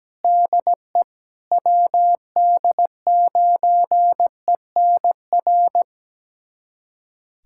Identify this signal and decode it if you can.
Morse code, heard as a keyed tone, about 17 wpm, 700 Hz. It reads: DE WD9ENR
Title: USA Callsign DE Callsign - 6 Characters - 17wpm